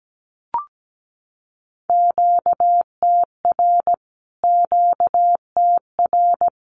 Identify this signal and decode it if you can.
Morse code, heard as a keyed tone, about 17 wpm, 700 Hz. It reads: QTR QTR
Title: Q-Codes Repeated 3x 17wpm